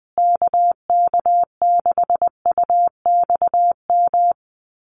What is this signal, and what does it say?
Morse code, heard as a keyed tone, about 20 wpm, 700 Hz. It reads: KK6UXM
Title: USA Call Signs Repeated 2x 20wpm